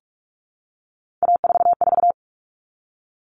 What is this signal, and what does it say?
Morse code, heard as a keyed tone, about 44 wpm, 700 Hz. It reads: A44
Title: All Cut Numbers - 3-Digit 45wpm